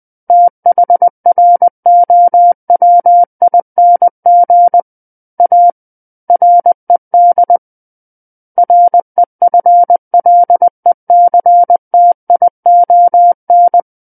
Morse code, heard as a keyed tone, about 20 wpm, 700 Hz. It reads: THROWING A RED REFLECTION
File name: 20 WPM